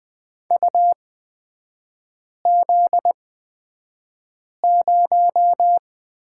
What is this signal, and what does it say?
Morse code, heard as a keyed tone, about 20 wpm, spaced as for 5 wpm, 700 Hz. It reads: UZ0
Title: 243 Lesson 39e - Letter z - Three Characters 20wpm